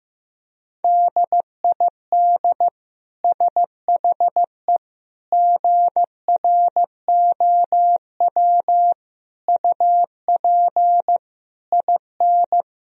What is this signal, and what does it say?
Morse code, heard as a keyed tone, about 15 wpm, 700 Hz. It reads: DID SHE GROW UP IN
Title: Sentences from Top 1000 Words - Review Part 2 - 15wpm